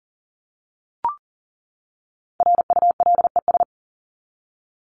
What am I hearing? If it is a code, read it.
Morse code, heard as a keyed tone, about 37 wpm, 700 Hz. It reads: RULES
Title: ES:S Suffix - Top 100 Words with Base Words 40wpm